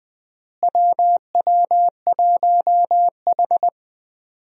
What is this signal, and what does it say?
Morse code, heard as a keyed tone, about 21 wpm, 700 Hz. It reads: WW1H
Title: USA Call Signs - 4 Characters - Repeated 3x - 20wpm